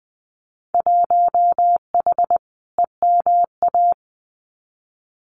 Morse code, heard as a keyed tone, about 20 wpm, 700 Hz. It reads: 1H EMA
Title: Winter Field Day Exchanges 20wpm